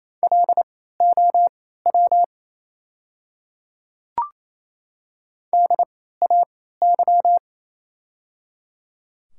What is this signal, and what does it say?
Morse code, heard as a keyed tone, about 28 wpm, spaced as for 16 wpm, 700 Hz. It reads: LOW DAY
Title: 3x Character Spacing - Top 200 Words - 28wpm